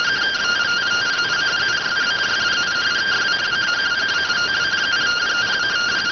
Two-tone teleprinter signal, about 170 Hz shift.